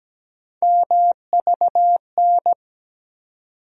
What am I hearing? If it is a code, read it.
Morse code, heard as a keyed tone, about 17 wpm, 700 Hz. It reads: MVN
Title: USA Call Sign Postfixes - 3 Letters 17wpm